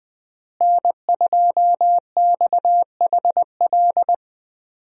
Morse code, heard as a keyed tone, about 21 wpm, 700 Hz. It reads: N2XHL
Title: USA Call Signs - 5 Characters - Repeated 2x - 20wpm